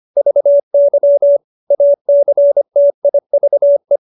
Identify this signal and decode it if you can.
Morse code, heard as a keyed tone, about 25 wpm, 570 Hz. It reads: VY ACTIVE